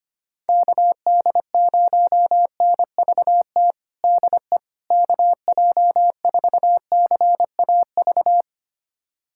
Morse code, heard as a keyed tone, about 25 wpm, 700 Hz. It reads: KD0NVT DE KJ4CAV
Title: USA Callsign DE Callsign 25wpm